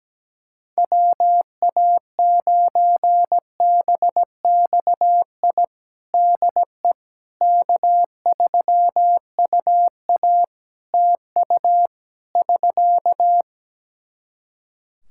Morse code, heard as a keyed tone, about 17 wpm, 700 Hz. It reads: WA9BXI DE K3UA TU <SK>
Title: K1USN SST QSO Practice 17wpm